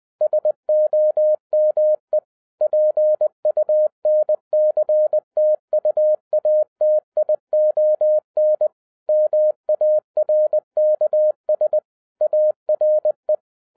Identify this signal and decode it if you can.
Morse code, heard as a keyed tone, about 20 wpm, 610 Hz. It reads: SOME PUNCTUATION MARKS ARE